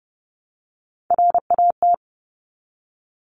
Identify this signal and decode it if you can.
Morse code, heard as a keyed tone, about 30 wpm, 700 Hz. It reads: RAT
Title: Top Three Letter Combinations in Words 30wpm